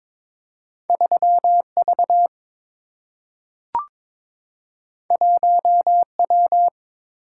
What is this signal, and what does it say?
Morse code, heard as a keyed tone, about 22 wpm, 700 Hz. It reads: 3 V 1 W
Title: Single-Letter Single-Number 22wpm